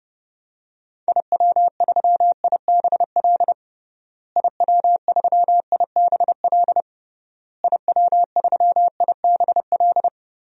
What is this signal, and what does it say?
Morse code, heard as a keyed tone, about 32 wpm, 700 Hz. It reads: IW3IBL IW3IBL IW3IBL
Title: DXCC-100 Call Signs - 6 Characters - Repeated 3x - 30wpm